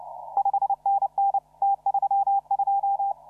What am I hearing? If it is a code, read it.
Morse code, heard as a keyed tone, about 33 wpm, 780 Hz. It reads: 5NN T32
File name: CW / RHOB CW 2017 HA6P 0001